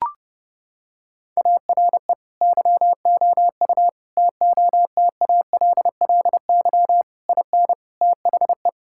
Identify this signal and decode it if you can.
Morse code, heard as a keyed tone, about 32 wpm, 700 Hz. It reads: ARE YOU TOTALLY IN THE
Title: Sentences from Top 1600 Words 30wpm